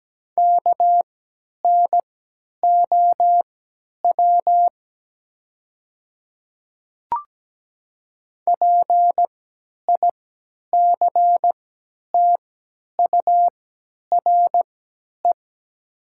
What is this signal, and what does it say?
Morse code, heard as a keyed tone, about 17 wpm, spaced as for 10 wpm, 700 Hz. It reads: KNOW PICTURE
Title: 3x Character Spacing - Top 200 Words - 17wpm